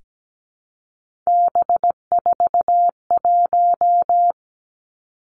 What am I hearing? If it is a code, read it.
Morse code, heard as a keyed tone, about 17 wpm, 700 Hz. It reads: B41
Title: Three Challenging Letter-Number Combinations 17wpm